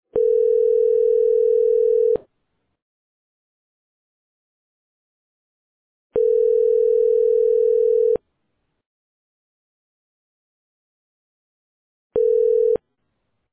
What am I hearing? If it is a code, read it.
Ringback tone (North America), 440 + 480 Hz.